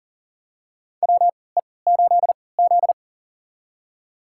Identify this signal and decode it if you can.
Morse code, heard as a keyed tone, about 37 wpm, spaced as for 22 wpm, 700 Hz. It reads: WE8Z
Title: USA Call Signs - 4 Characters - 3x Character Spacing - 40wpm